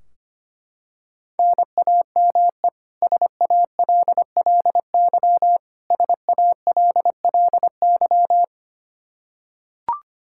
Morse code, heard as a keyed tone, about 25 wpm, 700 Hz. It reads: NAME SALLY SALLY
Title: K1USN SST QSO Practice 25wpm